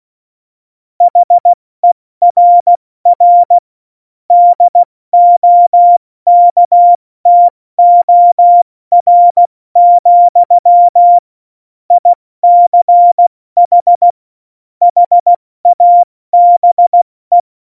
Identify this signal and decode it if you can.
Morse code, heard as a keyed tone, about 16 wpm, 700 Hz. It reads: HERR DOKTOR, ICH HABE
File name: Morsekurs / Witze / Witz 26. Dezember 2017, 70 F80